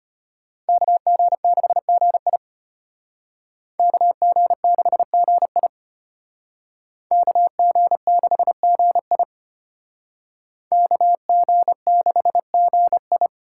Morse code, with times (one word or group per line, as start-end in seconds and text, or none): 0.7-2.4 KG6GI
3.8-5.7 KG6GI
7.1-9.2 KG6GI
10.7-13.3 KG6GI